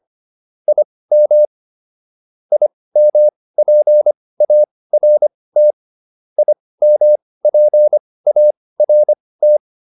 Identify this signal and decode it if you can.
Morse code, heard as a keyed tone, about 25 wpm, spaced as for 18 wpm, 600 Hz. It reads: IM IMPART IMPART